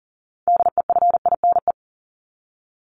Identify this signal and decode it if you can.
Morse code, heard as a keyed tone, about 40 wpm, 700 Hz. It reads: DEFINE
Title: ED Suffix - Top 100 Words with Base Words 40wpm